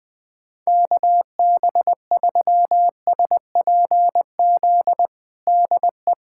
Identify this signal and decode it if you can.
Morse code, heard as a keyed tone, about 20 wpm, 700 Hz. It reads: KB3SPZ DE
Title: USA Callsign DE Callsign - 6 Characters - 20wpm